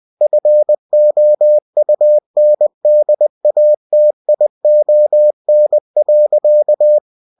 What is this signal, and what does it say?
Morse code, heard as a keyed tone, about 20 wpm, 600 Hz. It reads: FOUNDATION.
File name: cw-books / winnie-the-pooh / winnie-20wpm-page-0117